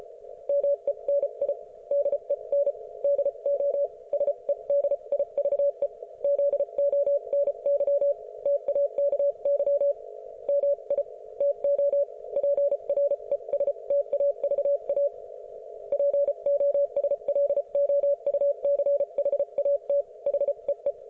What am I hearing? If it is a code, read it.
Morse code, heard as a keyed tone, about 34 wpm, spaced as for 30 wpm, 570 Hz. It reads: MENI DEN DO SEDIVE ZONY TAKY MI TO PRESTAVA POSLOUCHAT HEE